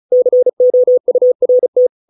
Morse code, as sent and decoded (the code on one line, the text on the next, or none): -.-. --- ..- .-. -
COURT